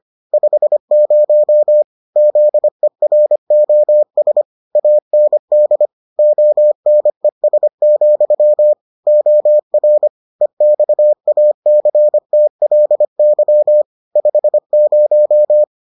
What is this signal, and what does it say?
Morse code, heard as a keyed tone, about 25 wpm, 600 Hz. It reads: 50 ZEROS AND ONES, OR EXACTLY 50